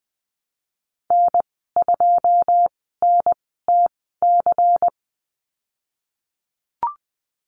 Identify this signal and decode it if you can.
Morse code, heard as a keyed tone, about 20 wpm, spaced as for 15 wpm, 700 Hz. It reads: N2NTC
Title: 122 Lesson 20g - Number 3 - Call Signs 20wpm